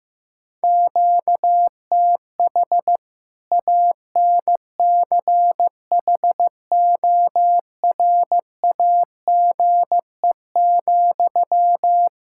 Morse code, heard as a keyed tone, about 16 wpm, 700 Hz. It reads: QTH ANCHORAGE,
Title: Practice QSO - Exchange 1 - Beginning - 15wpm